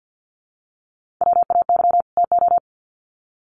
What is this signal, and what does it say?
Morse code, heard as a keyed tone, about 50 wpm, 700 Hz. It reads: WAY TO
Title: Sets of 2 Words 50wpm